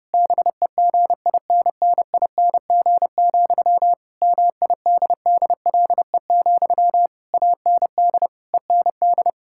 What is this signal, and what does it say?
Morse code, heard as a keyed tone, about 30 wpm, 700 Hz. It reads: BEGINNING, MIDDLE, AND END
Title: Sentences from Top 1000 Words - Review Part 6 - 30wpm